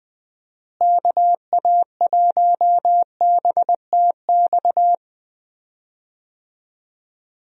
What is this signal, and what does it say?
Morse code, heard as a keyed tone, about 20 wpm, 700 Hz. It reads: KA1BTX
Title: USA Call Signs - 6 Characters - Repeated 2x - 20wpm